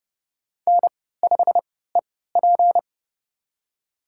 Morse code, heard as a keyed tone, about 30 wpm, spaced as for 17 wpm, 700 Hz. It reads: N5EP
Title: USA Call Signs - 4 Characters - 3x Character Spacing - 30wpm